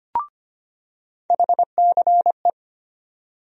Morse code, heard as a keyed tone, about 25 wpm, 700 Hz. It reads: HCE
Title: USA Call Sign Postfixes - 3 Letters 25wpm